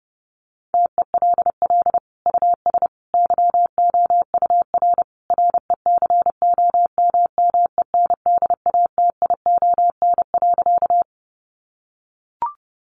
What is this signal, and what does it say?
Morse code, heard as a keyed tone, about 30 wpm, 700 Hz. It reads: TELL US YOUR RECOMMENDATION.
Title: Sentences from Top 1400 Words 30wpm